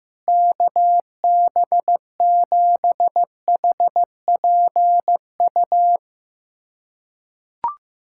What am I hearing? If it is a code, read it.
Morse code, heard as a keyed tone, about 15 wpm, 700 Hz. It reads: KB7HPU
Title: USA Callsign DE Callsign - 6 Characters - 15wpm